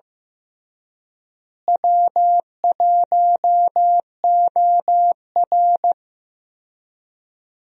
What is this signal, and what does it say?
Morse code, heard as a keyed tone, about 15 wpm, 700 Hz. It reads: W1OR
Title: USA Call Signs - 4 Characters - Repeated 3x - 15wpm